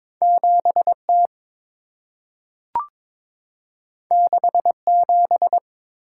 Morse code, heard as a keyed tone, about 22 wpm, 690 Hz. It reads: 7T 67
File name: practice / 22wpm / Two Letter-Number Combinations - 22wpm